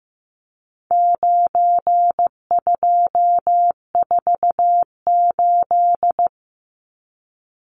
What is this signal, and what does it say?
Morse code, heard as a keyed tone, about 15 wpm, 700 Hz. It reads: 9248
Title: Numbers - 4-Digit 15wpm